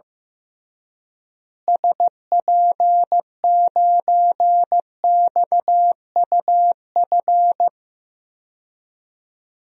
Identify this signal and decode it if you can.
Morse code, heard as a keyed tone, about 15 wpm, 700 Hz. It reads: SP9XUF
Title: DXCC-100 Call Signs - 6 Characters - Repeated 2x - 15wpm